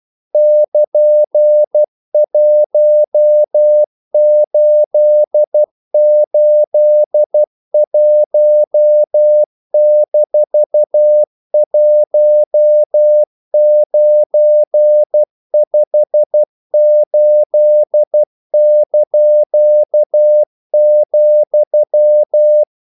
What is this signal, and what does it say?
Morse code, heard as a keyed tone, about 12 wpm, 600 Hz. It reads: <KN>1881-1958),
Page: English quotations in CW (telegraphy - morse)